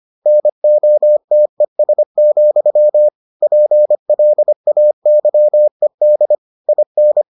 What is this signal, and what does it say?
Morse code, heard as a keyed tone, about 25 wpm, 600 Hz. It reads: NOTES, PLAYED IN